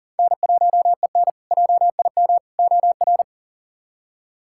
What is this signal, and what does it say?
Morse code, heard as a keyed tone, about 40 wpm, 700 Hz. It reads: N1EN JIM OR
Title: K1USN SST QSO Practice - 40wpm